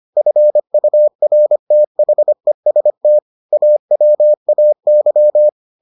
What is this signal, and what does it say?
Morse code, heard as a keyed tone, about 25 wpm, 600 Hz. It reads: FURTHEST AWAY